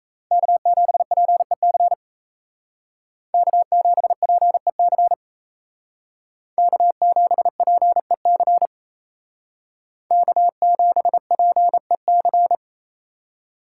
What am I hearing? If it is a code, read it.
Morse code, heard as a keyed tone, about 36 wpm, 700 Hz. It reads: K7PEC K7PEC K7PEC K7PEC